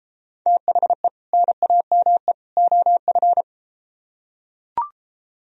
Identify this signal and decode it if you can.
Morse code, heard as a keyed tone, about 33 wpm, 700 Hz. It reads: THE NAME OF